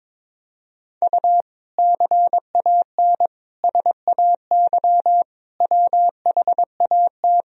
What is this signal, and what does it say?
Morse code, heard as a keyed tone, about 22 wpm, 700 Hz. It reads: U CAN SAY WHAT
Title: Sentences from Top 200 Words and Abbreviations 22wpm